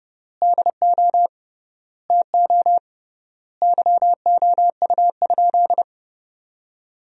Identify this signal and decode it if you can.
Morse code, heard as a keyed tone, about 32 wpm, 700 Hz. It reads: DO TO YOU?
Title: 3x Word-Spacing - Sentences from Top 100 Words - 30wpm